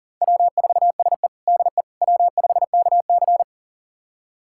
Morse code, heard as a keyed tone, about 40 wpm, 700 Hz. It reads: W4SE DE W5KC
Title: USA Callsign DE Callsign - 4 Characters - 40wpm